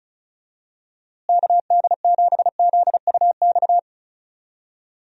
Morse code, heard as a keyed tone, about 36 wpm, 700 Hz. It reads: KD7ZUX
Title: USA Call Signs - 6 Characters - Repeated 2x - 35wpm